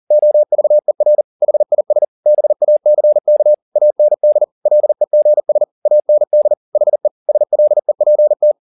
Morse code, heard as a keyed tone, about 40 wpm, 600 Hz. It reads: OVER HIS BACK AND LEGS AND HE SLEPT
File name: the old man and the sea 2 40wpm